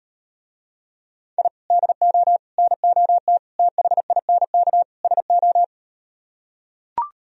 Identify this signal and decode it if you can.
Morse code, heard as a keyed tone, about 38 wpm, 700 Hz. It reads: I DO NOT THINK SO